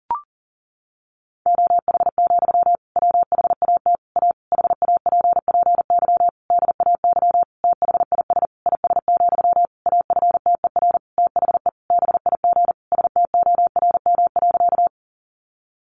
Morse code, heard as a keyed tone, about 40 wpm, 700 Hz. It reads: OH, WHAT A HAPPY DAY THIS IS, AFTER THE BIG STORM.
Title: Sentences from Top 1500 Words 40wpm